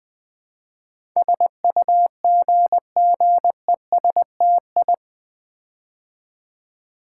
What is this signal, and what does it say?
Morse code, heard as a keyed tone, about 20 wpm, 700 Hz. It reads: SUGGESTI
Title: Build A Word - 4 Letters At A Time - 10-Letter Words - 20wpm